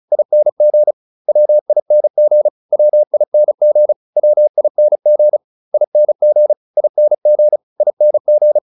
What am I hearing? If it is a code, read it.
Morse code, heard as a keyed tone, about 34 wpm, 600 Hz. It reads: ING WING WING WING ING ING ING